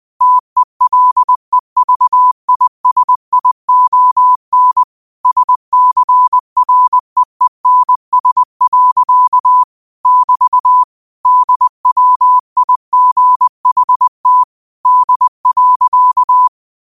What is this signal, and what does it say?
Morse code, heard as a keyed tone, about 20 wpm, 1000 Hz. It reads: TELEVISION SCREENS. <BT> DWIGHT D.